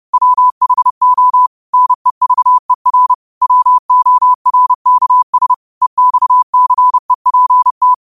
Morse code, heard as a keyed tone, about 30 wpm, 1000 Hz. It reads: WHO NEVER WORKS EXCEPT
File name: qotdfiles / 30wpm